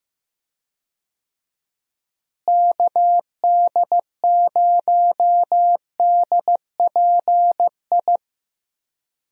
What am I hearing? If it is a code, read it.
Morse code, heard as a keyed tone, about 15 wpm, 700 Hz. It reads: KD0DPI
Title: USA Call Signs - 6 Characters - Repeated 2x - 15wpm